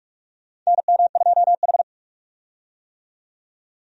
Morse code, heard as a keyed tone, about 46 wpm, 700 Hz. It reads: NM2H
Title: USA Call Signs - 4 Characters - 45wpm